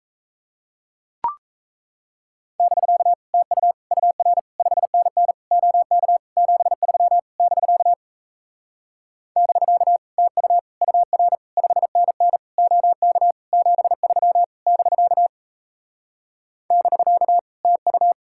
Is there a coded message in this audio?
Yes: Morse code from 2.6 to 18.1 s, heard as a keyed tone, about 38 wpm, 700 Hz. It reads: <BK> TU UR 5NN OK 73 <BK> <BK> TU UR 5NN OK 73 <BK> <BK> TU